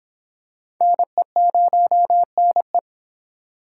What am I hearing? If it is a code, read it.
Morse code, heard as a keyed tone, about 25 wpm, 700 Hz. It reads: NE0NE